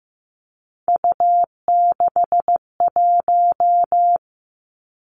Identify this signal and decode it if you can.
Morse code, heard as a keyed tone, about 15 wpm, 700 Hz. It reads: U61
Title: Three Challenging Letter-Number Combinations 15wpm